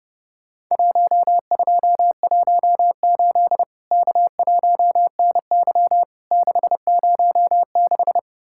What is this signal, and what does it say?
Morse code, heard as a keyed tone, about 30 wpm, 700 Hz. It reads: 1218 K1NY 606
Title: Worked All Europe DX Contest - QTC Exchange - 30wpm